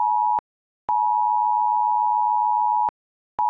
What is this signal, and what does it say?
Emergency Alert System (SAME) tones (the 853 and 960 Hz attention tone). It is heard from 0.8 to 3.0 s.
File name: ui / emergency alert